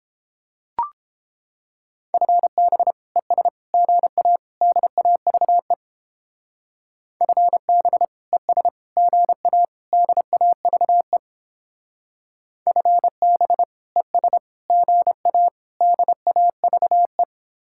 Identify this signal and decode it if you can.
Morse code, heard as a keyed tone, about 31 wpm, 700 Hz. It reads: FB ES GA DAVE FB ES GA DAVE FB ES GA DAVE